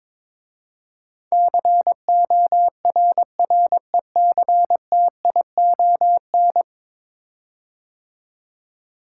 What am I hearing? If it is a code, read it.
Morse code, heard as a keyed tone, about 22 wpm, 700 Hz. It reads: CORRECTION
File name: practice / 22wpm / 10 Letter Words - First 3 Letters Given - 22wpm